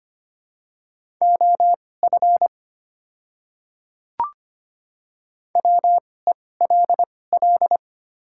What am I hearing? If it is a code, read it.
Morse code, heard as a keyed tone, about 25 wpm, spaced as for 18 wpm, 700 Hz. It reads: OF WELL
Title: 2x Character Spacing - Top 300 Words - 25wpm